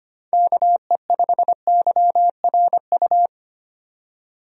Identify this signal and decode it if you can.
Morse code, heard as a keyed tone, about 25 wpm, 700 Hz. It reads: KE5YRU
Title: USA Call Signs - 6 Characters - 25wpm